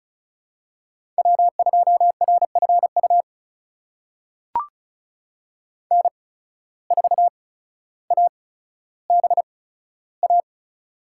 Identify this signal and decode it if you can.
Morse code, heard as a keyed tone, about 34 wpm, spaced as for 15 wpm, 700 Hz. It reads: W2RFU N4ABA
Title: USA Call Signs - 8x Character Spacing - 35wpm